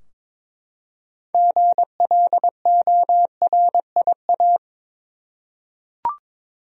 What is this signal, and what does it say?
Morse code, heard as a keyed tone, about 22 wpm, 700 Hz. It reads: GLORIA
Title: USA Common Names - 22wpm